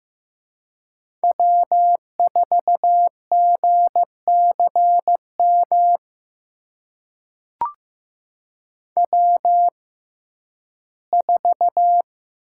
Morse code, heard as a keyed tone, about 16 wpm, spaced as for 11 wpm, 700 Hz. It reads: W4GCM W4
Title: USA Call Signs - 5 Characters - 6x Character Spacing - 15wpm